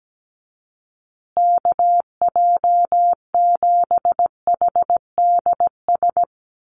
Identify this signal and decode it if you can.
Morse code, heard as a keyed tone, about 17 wpm, 700 Hz. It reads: KJ7HDS
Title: USA Call Signs Repeated 3x 17wpm